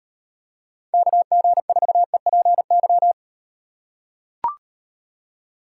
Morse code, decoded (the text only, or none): KG4EPY